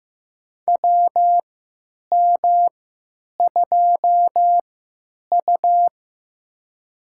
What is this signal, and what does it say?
Morse code, heard as a keyed tone, about 15 wpm, spaced as for 9 wpm, 700 Hz. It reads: WM2U
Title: USA Call Signs - 4 Characters - 3x Character Spacing - 15wpm